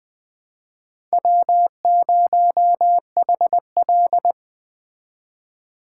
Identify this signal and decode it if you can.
Morse code, heard as a keyed tone, about 21 wpm, 700 Hz. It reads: W0HL
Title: Speed Racing - USA 1x2 Call Signs 17wpm